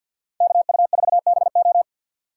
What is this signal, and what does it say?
Morse code, heard as a keyed tone, about 46 wpm, 700 Hz. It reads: KU4BO
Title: USA Call Signs - 5 Characters - 4x Character Spacing - 50wpm